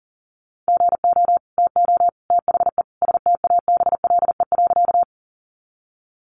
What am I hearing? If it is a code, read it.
Morse code, heard as a keyed tone, about 37 wpm, 700 Hz. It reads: GO TO THE STABLE.
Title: Sentences from Top 1500 Words 40wpm